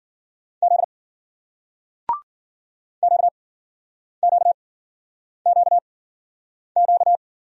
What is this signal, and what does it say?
Morse code, heard as a keyed tone, about 54 wpm, spaced as for 9 wpm, 700 Hz. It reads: Z QQQQ